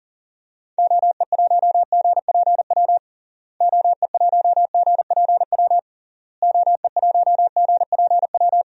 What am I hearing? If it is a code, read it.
Morse code, heard as a keyed tone, about 40 wpm, 700 Hz. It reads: OE1GPW OE1GPW OE1GPW
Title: DXCC-100 Call Signs - 6 Characters - Repeated 3x - 40wpm